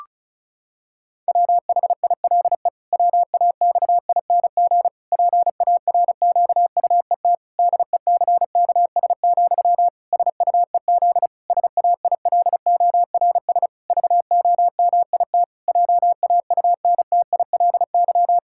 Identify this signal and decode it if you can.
Morse code, heard as a keyed tone, about 34 wpm, 700 Hz. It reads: WHILE WAXING PARQUET DECKS, SUEZ SAILORS VOMIT JAUNTILY
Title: Pangram Challenge v2 - 35wpm